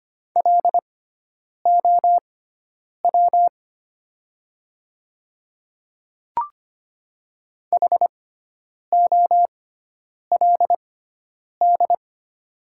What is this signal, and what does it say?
Morse code, heard as a keyed tone, about 25 wpm, spaced as for 9 wpm, 700 Hz. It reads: LOW HOLD
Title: 6x Character Spacing - Top 500 Words - 25wpm